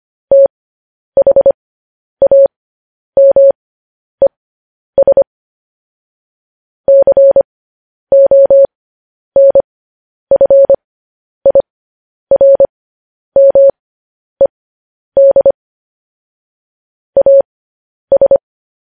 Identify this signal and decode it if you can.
Morse code, heard as a keyed tone, about 25 wpm, spaced as for 10 wpm, 550 Hz. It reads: THAMES CONFIRMED AS
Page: News Headlines in Morse Code at 10 WPM